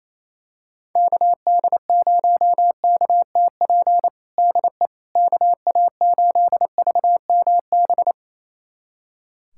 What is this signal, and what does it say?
Morse code, heard as a keyed tone, about 28 wpm, 700 Hz. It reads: KD0KTP DE KA8VMB
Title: USA Callsign DE Callsign 28wpm